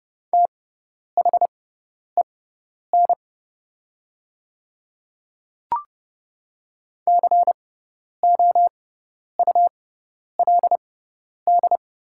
Morse code, heard as a keyed tone, about 30 wpm, spaced as for 10 wpm, 700 Hz. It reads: THEN COULD
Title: 6x Character Spacing - Top 200 Words - 30wpm